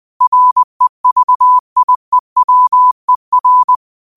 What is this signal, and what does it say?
Morse code, heard as a keyed tone, about 20 wpm, 1000 Hz. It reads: REVIEWER
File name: 20wpm